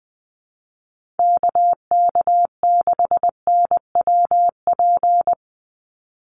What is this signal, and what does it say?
Morse code, heard as a keyed tone, about 20 wpm, 700 Hz. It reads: KK6NWP
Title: USA Call Signs 20wpm